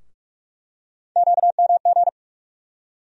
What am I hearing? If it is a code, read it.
Morse code, heard as a keyed tone, about 46 wpm, 700 Hz. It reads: QMG
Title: USA Call Sign Postfixes - 3 Letters 45wpm